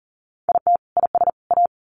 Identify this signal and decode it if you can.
Morse code, heard as a keyed tone, about 40 wpm, 700 Hz. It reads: IT IS A
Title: Sets of 3 Words 40wpm